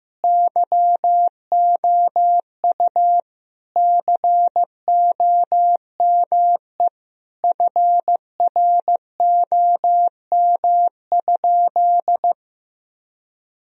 Morse code, heard as a keyed tone, about 16 wpm, 700 Hz. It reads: YOU COME FROM?
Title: Sentences from Top 1000 Words - Review Part 2 - 15wpm